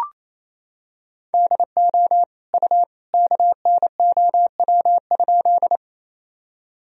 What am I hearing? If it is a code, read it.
Morse code, heard as a keyed tone, about 28 wpm, 700 Hz. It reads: DO U KNOW?
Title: Sentences from Top 100 Words and Abbreviations 28wpm